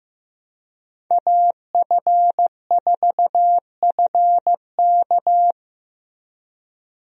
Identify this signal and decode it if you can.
Morse code, heard as a keyed tone, about 15 wpm, 700 Hz. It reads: AF4FK
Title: USA Call Signs - 5 Characters - Repeated 3x - 15wpm